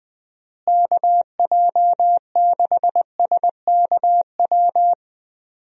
Morse code, heard as a keyed tone, about 20 wpm, 700 Hz. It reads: KJ6SKW
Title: USA Call Signs - 6 Characters - Repeated 3x - 20wpm